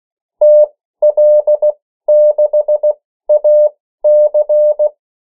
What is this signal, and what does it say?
Morse code, heard as a keyed tone, about 16 wpm, spaced as for 13 wpm, 600 Hz. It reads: TL6AC
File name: morse mp3 / 14 urok 13